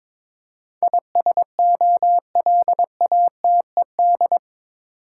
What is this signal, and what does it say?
Morse code, heard as a keyed tone, about 21 wpm, 700 Hz. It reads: ISOLATED
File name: practice / 22wpm / Eight Letter Words - 22wpm